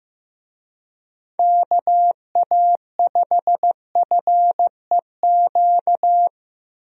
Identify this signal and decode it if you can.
Morse code, heard as a keyed tone, about 15 wpm, 700 Hz. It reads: KA5FEQ
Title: USA Call Signs - 6 Characters - Repeated 3x - 15wpm